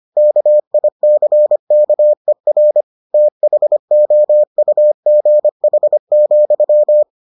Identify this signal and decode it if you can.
Morse code, heard as a keyed tone, about 25 wpm, 600 Hz. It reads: KICKER THOUGH,